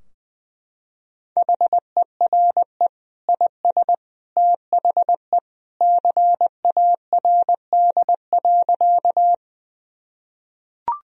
Morse code, heard as a keyed tone, about 20 wpm, 700 Hz. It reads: HERE IS THE CARD.
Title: Sentences from Top 1000 Words - Review Part 4 - 20wpm